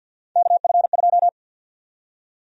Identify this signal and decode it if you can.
Morse code, heard as a keyed tone, about 48 wpm, 700 Hz. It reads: KF2
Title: USA Call Sign Prefixes - Repeated 2x - 50wpm